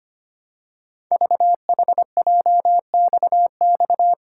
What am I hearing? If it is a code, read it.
Morse code, heard as a keyed tone, about 25 wpm, 700 Hz. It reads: VHJXX
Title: Code Groups - 5 Letters - 25wpm